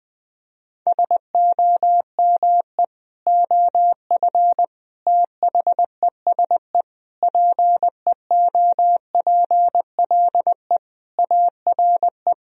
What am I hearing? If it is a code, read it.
Morse code, heard as a keyed tone, about 21 wpm, 700 Hz. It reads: SOME OF THESE PEOPLE ARE
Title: Sentences from Top 1000 Words - Review Part 3 - 20wpm